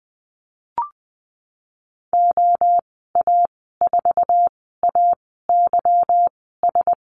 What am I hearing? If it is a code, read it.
Morse code, heard as a keyed tone, about 20 wpm, spaced as for 14 wpm, 700 Hz. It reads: OA4AYS
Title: 161 Lesson 26g - Letter g - Call Signs 20wpm